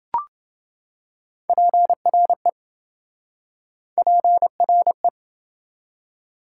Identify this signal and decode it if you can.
Morse code, heard as a keyed tone, about 28 wpm, 700 Hz. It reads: PRE PRE